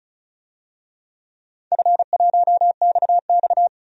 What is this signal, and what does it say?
Morse code, heard as a keyed tone, about 34 wpm, 700 Hz. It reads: F1XX
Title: DXCC-100 Call Signs - 4 Characters - 35wpm